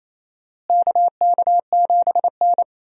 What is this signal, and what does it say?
Morse code, heard as a keyed tone, about 28 wpm, 700 Hz. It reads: KK7N
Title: USA Call Signs - 4 Characters - Repeated 3x - 28wpm